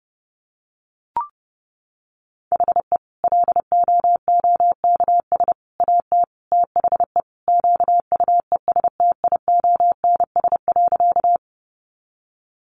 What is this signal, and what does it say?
Morse code, heard as a keyed tone, about 30 wpm, 700 Hz. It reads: HE LOOKS AT THE QUESTIONS.
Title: Additional Easy Phrases 30wpm